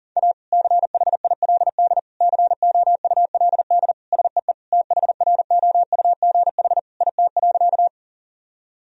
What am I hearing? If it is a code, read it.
Morse code, heard as a keyed tone, about 37 wpm, 700 Hz. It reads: A CHILD COULD SEE THROUGH IT.
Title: Sentences from Top 1000 Words - Review Part 1 - 40wpm